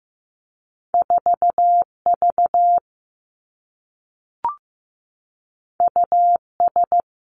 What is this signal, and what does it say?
Morse code, heard as a keyed tone, about 15 wpm, 700 Hz. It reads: 4 V U S
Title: Two Challenging Letter-Number Combinations 15wpm